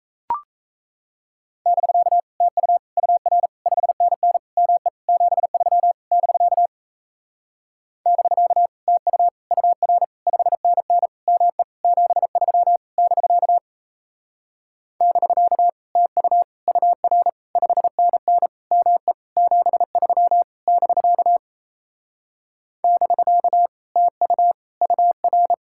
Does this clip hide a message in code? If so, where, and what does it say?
Yes: Morse code from 1.7 to 25.5 s, heard as a keyed tone, about 37 wpm, 700 Hz. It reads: <BK> TU UR 5NN ME 73 <BK> <BK> TU UR 5NN ME 73 <BK> <BK> TU UR 5NN ME 73 <BK> <BK> TU UR